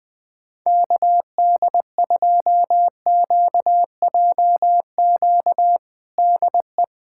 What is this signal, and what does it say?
Morse code, heard as a keyed tone, about 20 wpm, 700 Hz. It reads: KD2QJQ DE
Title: USA Callsign DE Callsign 20wpm